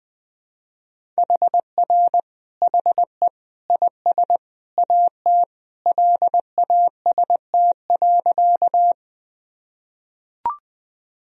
Morse code, heard as a keyed tone, about 20 wpm, 700 Hz. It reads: HR HE IS AT LAST.
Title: Sentences from Top 300 Words and Abbreviations 20wpm